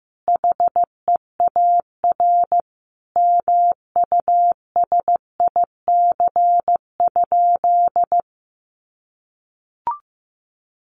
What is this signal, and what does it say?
Morse code, heard as a keyed tone, about 15 wpm, 700 Hz. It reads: HEAR MUSIC?
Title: Sentences from Top 300 Words 15wpm